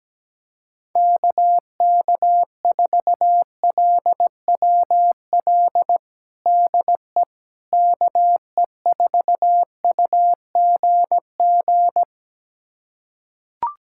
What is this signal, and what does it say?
Morse code, heard as a keyed tone, about 17 wpm, 700 Hz. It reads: KK4LWL DE KE4UGG
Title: USA Callsign DE Callsign - 6 Characters - 17wpm